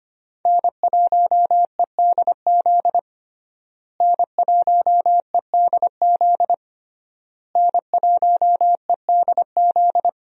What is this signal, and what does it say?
Morse code, heard as a keyed tone, about 25 wpm, 700 Hz. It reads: N1EDZ N1EDZ N1EDZ
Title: USA Call Signs - 5 Characters - Repeated 3x - 25wpm